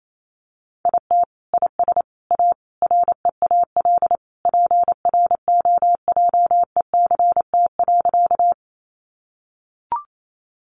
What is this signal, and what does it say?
Morse code, heard as a keyed tone, about 28 wpm, 700 Hz. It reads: IT IS A REAL PROJECT.
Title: Sentences from Top 1400 Words 28wpm